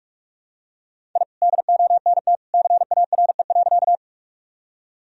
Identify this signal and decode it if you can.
Morse code, heard as a keyed tone, about 45 wpm, 700 Hz. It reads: I DONT CARE.
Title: Sentences from Top 1000 Words - Review Part 1 - 45wpm